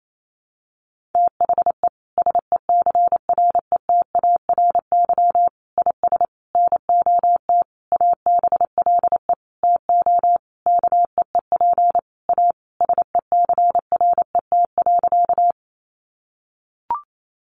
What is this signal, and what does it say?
Morse code, heard as a keyed tone, about 28 wpm, 700 Hz. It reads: THE SECRETARY IS NOT ABLE TO KEEP A SECRET.
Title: Sentences from Top 1300 Words 28wpm